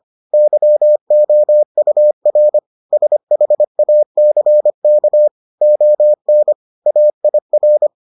Morse code, heard as a keyed tone, about 25 wpm, 600 Hz. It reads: YOUR SHACK ON AIR